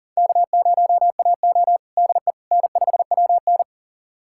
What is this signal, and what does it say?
Morse code, heard as a keyed tone, about 40 wpm, 700 Hz. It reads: K0AO DE N5WN
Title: USA Callsign DE Callsign - 4 Characters - 40wpm